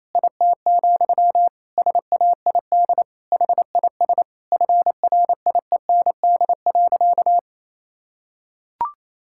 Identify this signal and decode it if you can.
Morse code, heard as a keyed tone, about 28 wpm, 700 Hz. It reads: IT, SAID HIS FRIEND.
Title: Sentences from Top 1000 Words - Review Part 5 - 28wpm